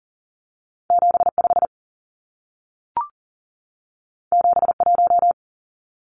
Morse code, heard as a keyed tone, about 40 wpm, 700 Hz. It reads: 75 71
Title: Numbers - 2-Digit 40wpm